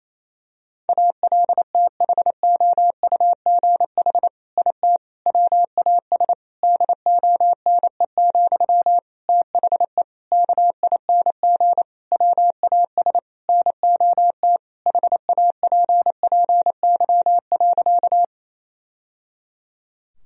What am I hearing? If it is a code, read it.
Morse code, heard as a keyed tone, about 27 wpm, 700 Hz. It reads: ALTHOUGH IT WAS DONE, THE KING WAS NOT HAPPY.
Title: Sentences from Top 1000 Words - Review Part 3 - 28wpm